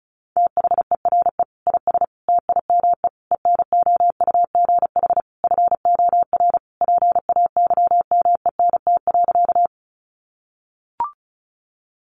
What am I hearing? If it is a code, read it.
Morse code, heard as a keyed tone, about 34 wpm, 700 Hz. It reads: THERE IS TIME ENOUGH FOR PAYMENT.
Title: Sentences from Top 1200 Words 35wpm